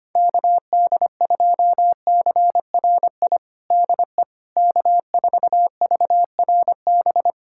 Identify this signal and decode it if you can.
Morse code, heard as a keyed tone, about 25 wpm, 700 Hz. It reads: KD2CRI DE K4VRB
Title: USA Callsign DE Callsign 25wpm